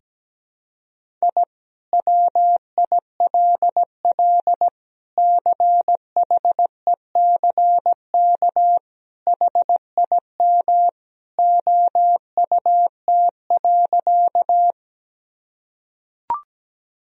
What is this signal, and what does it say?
Morse code, heard as a keyed tone, about 17 wpm, 700 Hz. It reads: I WILL CHECK HIM OUT.
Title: Sentences from Top 1000 Words - Review Part 3 - 17wpm